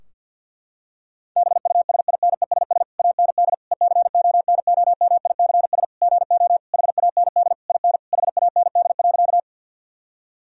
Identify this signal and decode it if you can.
Morse code, heard as a keyed tone, about 50 wpm, 700 Hz. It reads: BUSINESS AND ECONOMICS GO HAND IN HAND.
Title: Sentences from Top 1000 Words - Review Part 3 - 50wpm